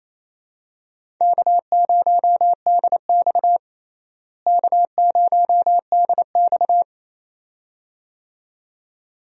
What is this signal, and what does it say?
Morse code, heard as a keyed tone, about 28 wpm, 700 Hz. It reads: K0DX K0DX
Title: USA Call Signs - 4 Characters - Repeated 2x - 28wpm